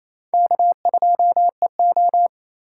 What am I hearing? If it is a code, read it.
Morse code, heard as a keyed tone, about 34 wpm, 700 Hz. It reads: K2EO
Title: Speed Racing - USA 1x2 Call Signs 28wpm